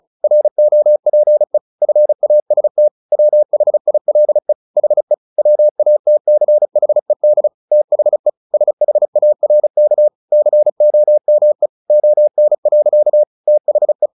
Morse code, heard as a keyed tone, about 34 wpm, 600 Hz. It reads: ROPE FAST WHILE HE WATCHED THE SHARK COME ON. THE
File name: the old man and the sea 2 35wpm